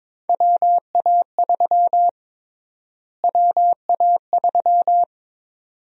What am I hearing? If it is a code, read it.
Morse code, heard as a keyed tone, about 21 wpm, 700 Hz. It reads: WA3 WA3
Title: USA Call Sign Prefixes - Repeated 2x - 22wpm